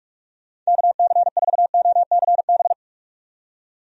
Morse code, heard as a keyed tone, about 46 wpm, 700 Hz. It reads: KC4OCB
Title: USA Call Signs - 6 Characters - 45wpm